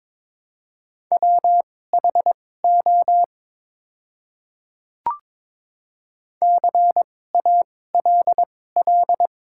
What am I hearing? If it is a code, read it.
Morse code, heard as a keyed tone, about 22 wpm, spaced as for 16 wpm, 700 Hz. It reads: WHO CALL
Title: 2x Character Spacing - Top 100 Words - 22wpm